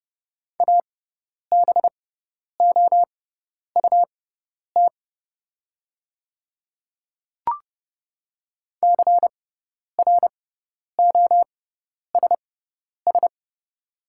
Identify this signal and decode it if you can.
Morse code, heard as a keyed tone, about 31 wpm, spaced as for 10 wpm, 700 Hz. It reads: ABOUT CROSS
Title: 6x Character Spacing - Top 300 Words - 30wpm